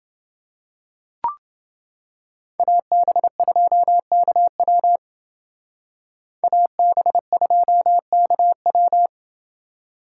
Morse code, heard as a keyed tone, about 28 wpm, 700 Hz. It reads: AB2KW AB2KW